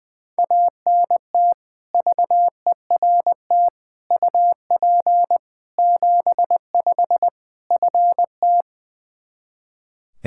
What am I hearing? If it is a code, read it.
Morse code, heard as a keyed tone, about 20 wpm, 700 Hz. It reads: ANT VERT UP 75 FT
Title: Practice QSO - Exchange 2 - Info - 20wpm